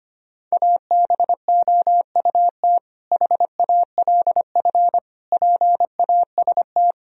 Morse code, heard as a keyed tone, about 25 wpm, 700 Hz. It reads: ABOUT HALF PAST
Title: Sentences from Top 1000 Words - Review Part 3 - 25wpm